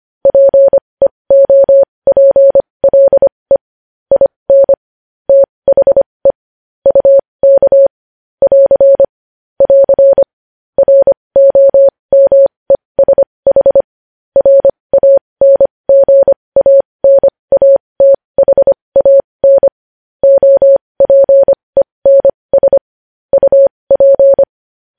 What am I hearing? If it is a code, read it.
Morse code, heard as a keyed tone, about 25 wpm, spaced as for 20 wpm, 550 Hz. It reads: PEOPLE IN THE UK <AR> <AR> ROMESH RANGANATHAN OPENS UP
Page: news-headlines-in-morse-code-at-20-wpm